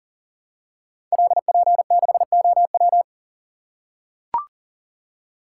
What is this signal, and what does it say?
Morse code, heard as a keyed tone, about 38 wpm, 700 Hz. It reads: LP6OW
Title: Code Groups - 5 Letter:Numbers - 40wpm